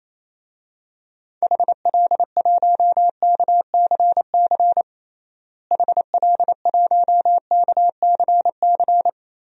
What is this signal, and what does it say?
Morse code, heard as a keyed tone, about 28 wpm, 700 Hz. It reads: HL1KCC HL1KCC
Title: DXCC-100 Call Signs - 6 Characters - Repeated 3x - 28wpm